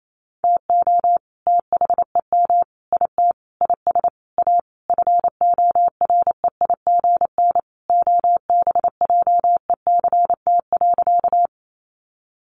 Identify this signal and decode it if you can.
Morse code, heard as a keyed tone, about 28 wpm, 700 Hz. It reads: TO THEM IT IS A FOREIGN OBJECT.
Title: Sentences from Top 1400 Words 28wpm